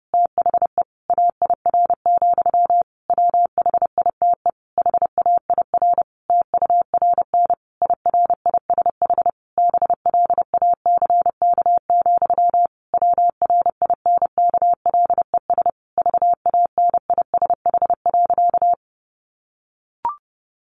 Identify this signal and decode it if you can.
Morse code, heard as a keyed tone, about 30 wpm, 700 Hz. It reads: THE AIR, WHITE HAIR TURN IRISH BLACK, WRINKLES VANISH.
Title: A Sound Of Thunder 30wpm